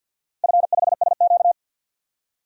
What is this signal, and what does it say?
Morse code, heard as a keyed tone, about 51 wpm, 700 Hz. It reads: F5SQ
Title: DXCC-100 Call Signs - 4 Characters - 50wpm